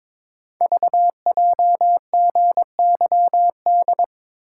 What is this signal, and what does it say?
Morse code, heard as a keyed tone, about 22 wpm, 700 Hz. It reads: VJGYD
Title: Code Groups - 5 Letters - 22wpm